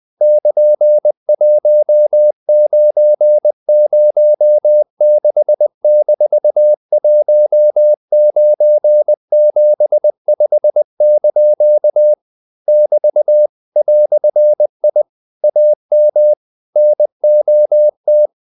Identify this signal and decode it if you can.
Morse code, heard as a keyed tone, about 20 wpm, 600 Hz. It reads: <KN>1906-1975) <BT> "I AM NOT
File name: CW / Practice / EnglishQuotations0320